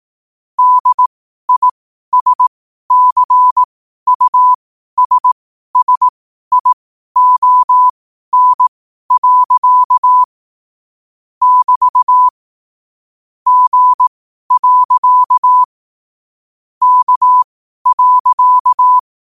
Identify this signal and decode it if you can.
Morse code, heard as a keyed tone, about 18 wpm, spaced as for 12 wpm, 1000 Hz. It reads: DISCUSSION. <BT> G. K.
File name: qotdfiles / 12wpm